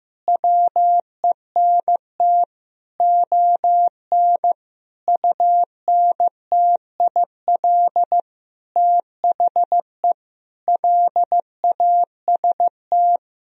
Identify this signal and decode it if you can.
Morse code, heard as a keyed tone, about 16 wpm, 700 Hz. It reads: WENT ON UNTIL THE LAST
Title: Sentences from Top 1000 Words - Review Part 1 - 15wpm